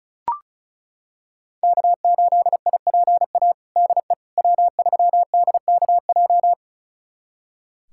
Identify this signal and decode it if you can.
Morse code, heard as a keyed tone, about 36 wpm, 700 Hz. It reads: K8IPA DE W3DKJ
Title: USA Callsign DE Callsign - 5 Characters - 35wpm